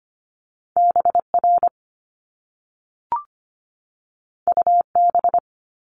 Morse code, heard as a keyed tone, about 25 wpm, 700 Hz. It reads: BR UB
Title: Top Two Letter Combinations in Words 25wpm